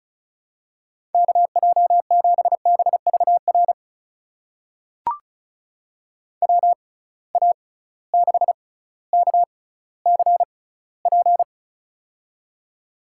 Morse code, heard as a keyed tone, about 36 wpm, spaced as for 18 wpm, 700 Hz. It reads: KJ7BVR WA6KCP
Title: USA Call Signs - 6 Characters - 6x Character Spacing - 35wpm